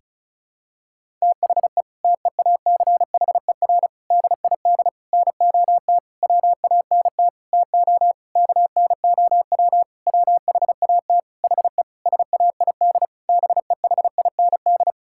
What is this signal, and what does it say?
Morse code, heard as a keyed tone, about 36 wpm, 700 Hz. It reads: THE TEACHER DID NOT WANT TO KNOW WHAT HE SAID BEHIND
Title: Sentences from Top 1000 Words - Review Part 6 - 35wpm